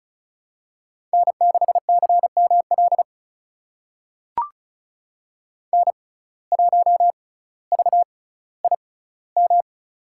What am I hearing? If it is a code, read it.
Morse code, heard as a keyed tone, about 34 wpm, spaced as for 18 wpm, 700 Hz. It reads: N6CML N1VIM
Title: USA Call Signs - 5 Characters - 6x Character Spacing - 35wpm